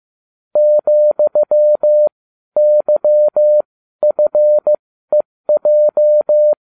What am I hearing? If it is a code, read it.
Morse code, heard as a keyed tone, about 15 wpm, spaced as for 12 wpm, 600 Hz. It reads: ,YFEJ